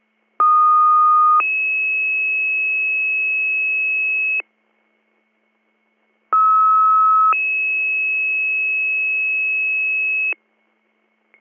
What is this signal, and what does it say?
Two-tone page, heard as a one-second tone, then a three-second tone, about 1230 then 2600 Hz, about 1290 then 2600 Hz. It is heard from 0.4 to 10.3 s.